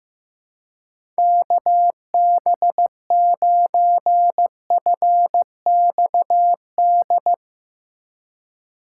Morse code, heard as a keyed tone, about 16 wpm, 700 Hz. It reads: KB9FXD
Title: USA Call Signs - 3x Character Spacing - 15wpm